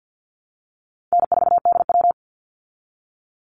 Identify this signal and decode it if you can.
Morse code, heard as a keyed tone, about 48 wpm, 700 Hz. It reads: N4DW
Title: Easy Call Signs Repeated 3x - 50wpm